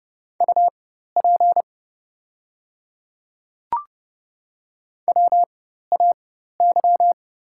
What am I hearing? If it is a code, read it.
Morse code, heard as a keyed tone, about 30 wpm, spaced as for 14 wpm, 700 Hz. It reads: UP WAY
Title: 4x Character Spacing - Top 300 Words - 30wpm